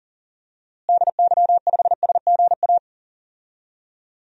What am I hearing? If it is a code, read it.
Morse code, heard as a keyed tone, about 40 wpm, 700 Hz. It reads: DY5SGA
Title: DXCC-100 Call Signs - 6 Characters - Repeated 2x - 40wpm